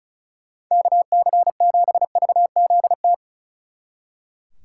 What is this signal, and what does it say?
Morse code, heard as a keyed tone, about 46 wpm, 700 Hz. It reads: KC7VZT